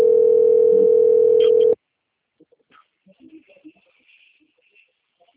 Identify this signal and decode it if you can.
Ringback tone (North America), 440 + 480 Hz.
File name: out- - - - - .524